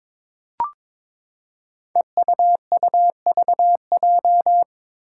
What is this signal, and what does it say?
Morse code, heard as a keyed tone, about 22 wpm, 700 Hz. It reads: EUUVJ
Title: Five Challenging Letter-Number Combinations 22wpm